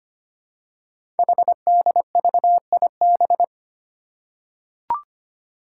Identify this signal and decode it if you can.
Morse code, heard as a keyed tone, about 25 wpm, 700 Hz. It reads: HDVIB
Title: Code Groups - 5 Letters - 25wpm